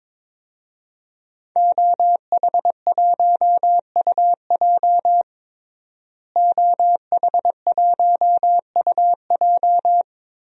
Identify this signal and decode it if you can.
Morse code, heard as a keyed tone, about 21 wpm, 700 Hz. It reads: OH1UJ OH1UJ
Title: DXCC-100 Call Signs - 5 Characters - Repeated 3x - 22wpm